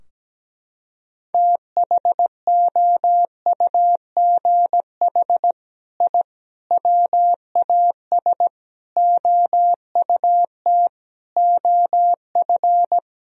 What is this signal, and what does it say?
Morse code, heard as a keyed tone, about 17 wpm, 700 Hz. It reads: THOUGH I WAS OUT OF
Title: Sentences from Top 1000 Words - Review Part 4 - 17wpm